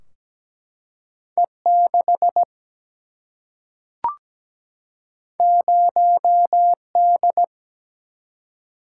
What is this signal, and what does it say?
Morse code, heard as a keyed tone, about 17 wpm, 700 Hz. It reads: E6 0D
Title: Single-Letter Single-Number 17wpm